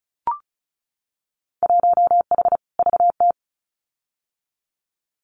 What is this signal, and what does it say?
Morse code, heard as a keyed tone, about 36 wpm, 700 Hz. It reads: 1H VT
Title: Winter Field Day Exchanges 35wpm